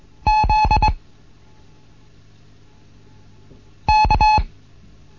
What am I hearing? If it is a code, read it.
Morse code, heard as a keyed tone, about 21 wpm, 860 Hz. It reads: Z K
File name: Morsen / Kurs A 15